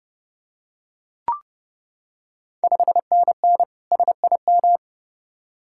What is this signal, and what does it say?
Morse code, heard as a keyed tone, about 30 wpm, 700 Hz. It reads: 5NN SIM
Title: Ontario QSO Party Exchange 30wpm